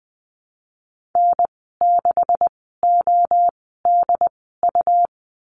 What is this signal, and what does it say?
Morse code, heard as a keyed tone, about 20 wpm, spaced as for 15 wpm, 700 Hz. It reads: N6ODU
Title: 128 Lesson 21g - Number 6 - Call Signs 20wpm